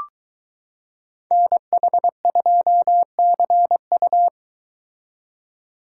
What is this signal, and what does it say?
Morse code, heard as a keyed tone, about 22 wpm, 700 Hz. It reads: NH2CU